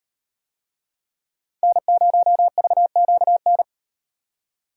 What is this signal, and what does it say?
Morse code, heard as a keyed tone, about 36 wpm, 700 Hz. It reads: N0VQN